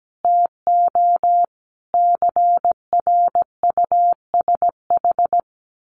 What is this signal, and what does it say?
Morse code, heard as a keyed tone, about 17 wpm, 700 Hz. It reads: TO CRUSH
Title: A Sound Of Thunder 17wpm